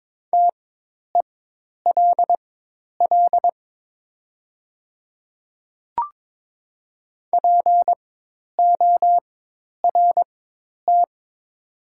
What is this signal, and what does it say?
Morse code, heard as a keyed tone, about 22 wpm, spaced as for 10 wpm, 700 Hz. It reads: TELL PORT
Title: 4x Character Spacing - Top 200 Words - 22wpm